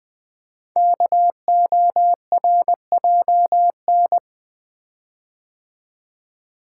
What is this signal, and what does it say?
Morse code, heard as a keyed tone, about 20 wpm, 700 Hz. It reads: KORJN
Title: Code Groups - 5 Letters - 20wpm